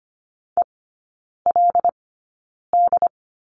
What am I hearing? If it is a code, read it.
Morse code, heard as a keyed tone, about 25 wpm, spaced as for 9 wpm, 700 Hz. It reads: ELD
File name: Lesson 17e - Letter u - Three Characters 25wpm